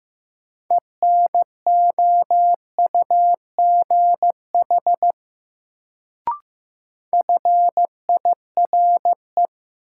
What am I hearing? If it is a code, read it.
Morse code, heard as a keyed tone, about 15 wpm, 700 Hz. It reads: ENOUGH FIRE
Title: Sentences from Top 400 Words - New Word Drill - 15wpm